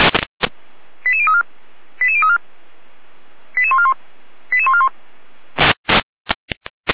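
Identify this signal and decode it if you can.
Five-tone selective call (ZVEI), heard as quick runs of five tones, tones of about 70 ms. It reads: 88034 88034 88141 88141
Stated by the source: ZVEI five-tone selective call received by SDR, logged on the Feuerwehr Orlamünde server.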